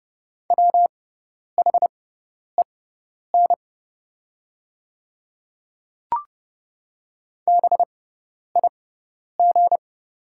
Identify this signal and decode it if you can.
Morse code, heard as a keyed tone, about 30 wpm, spaced as for 10 wpm, 700 Hz. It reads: WHEN BIG
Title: 6x Character Spacing - Top 200 Words - 30wpm